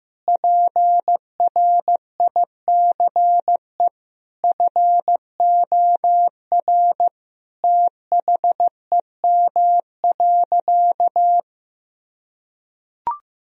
Morse code, heard as a keyed tone, about 15 wpm, 700 Hz. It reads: PRICE FOR THEM.
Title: Sentences from Top 1000 Words - Review Part 4 - 15wpm